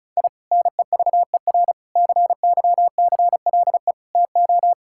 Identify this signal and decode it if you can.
Morse code, heard as a keyed tone, about 36 wpm, 700 Hz. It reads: I NEVER CYCLE TO
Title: Sentences from Top 1000 Words - Review Part 6 - 35wpm